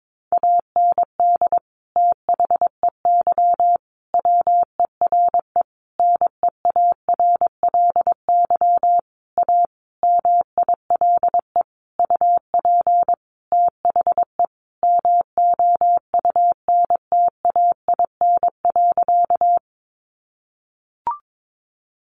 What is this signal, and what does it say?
Morse code, heard as a keyed tone, about 22 wpm, 700 Hz. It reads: AND THEY WERE NEARLY A MILE UP THE MOUNTAIN.
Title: Sentences from Top 1300 Words 22wpm